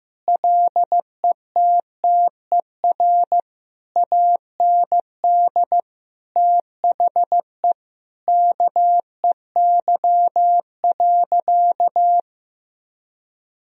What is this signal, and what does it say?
Morse code, heard as a keyed tone, about 16 wpm, 700 Hz. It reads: LETTER AND THE KEY.
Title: Sentences from Top 1000 Words - Review Part 4 - 15wpm